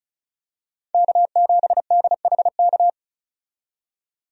Morse code, heard as a keyed tone, about 36 wpm, 700 Hz. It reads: K7DHK
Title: USA Call Signs - 5 Characters - 35wpm